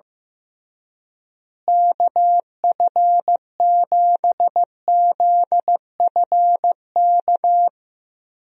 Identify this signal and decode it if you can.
Morse code, heard as a keyed tone, about 15 wpm, 700 Hz. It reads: KF7ZFK
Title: USA Call Signs - 6 Characters - Repeated 3x - 15wpm